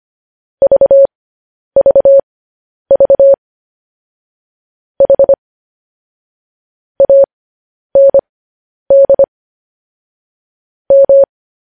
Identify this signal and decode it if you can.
Morse code, heard as a keyed tone, about 25 wpm, spaced as for 10 wpm, 550 Hz. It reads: VVV H AND M
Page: news-headlines-in-morse-code-at-10-wpm